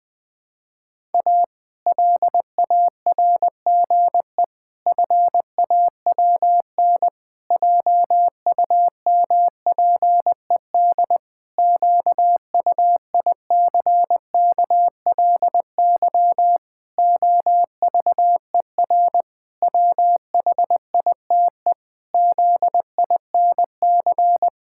Morse code, heard as a keyed tone, about 20 wpm, 700 Hz. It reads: A LARGE FAWN JUMPED QUICKLY OVER WHITE ZINC
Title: Pangram Challenge v2 - 20wpm